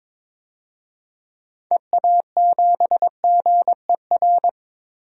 Morse code, heard as a keyed tone, about 22 wpm, 700 Hz. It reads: EA7GER
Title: DXCC-100 Call Signs - 6 Characters - 22wpm